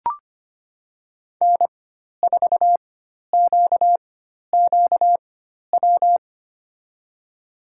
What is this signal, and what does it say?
Morse code, heard as a keyed tone, about 25 wpm, spaced as for 12 wpm, 700 Hz. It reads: N4QQW
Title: USA Call Signs - 5 Characters - 4x Character Spacing - 25wpm